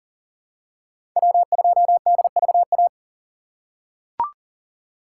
Morse code, heard as a keyed tone, about 40 wpm, 700 Hz. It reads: W2DVA
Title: USA Call Signs - 5 Characters - 2x Character Spacing - 40wpm